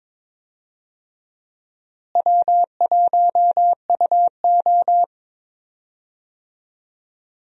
Morse code, heard as a keyed tone, about 22 wpm, 700 Hz. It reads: W1UO
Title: USA Call Signs - 4 Characters - Repeated 2x - 22wpm